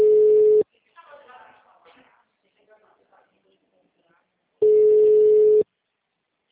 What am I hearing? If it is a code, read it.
Ringback tone (Europe, 425 Hz), with speech.
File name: out- -2044- - - .619